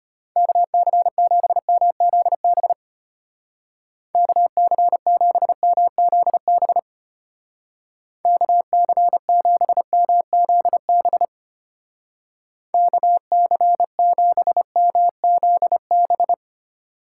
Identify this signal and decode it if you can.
Morse code, heard as a keyed tone, about 31 wpm, 700 Hz. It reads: KC7MZB KC7MZB KC7MZB KC7MZB
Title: Speed Racing - USA 2x3 Call Signs 25wpm